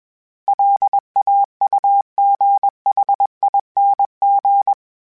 Morse code, heard as a keyed tone, about 21 wpm, 800 Hz. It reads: LAUGHING
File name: Special INGs / Special INGs-mp3s / Sngl Ing @20